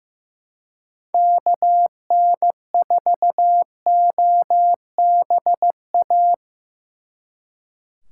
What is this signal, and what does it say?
Morse code, heard as a keyed tone, about 18 wpm, 700 Hz. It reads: KN4OBA
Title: Speed Racing - USA 2x3 Call Signs 15wpm